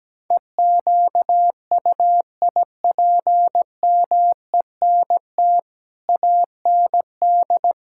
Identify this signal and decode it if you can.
Morse code, heard as a keyed tone, about 17 wpm, 700 Hz. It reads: EQUIPMENT AND
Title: Sentences from Top 1000 Words - Review Part 5 - 17wpm